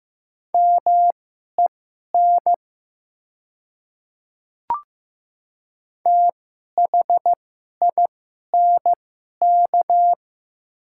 Morse code, heard as a keyed tone, about 15 wpm, spaced as for 11 wpm, 700 Hz. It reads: MEN THINK
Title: 2x Character Spacing - Top 200 Words - 15wpm